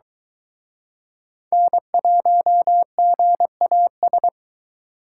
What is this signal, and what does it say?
Morse code, heard as a keyed tone, about 23 wpm, 700 Hz. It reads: N1GAS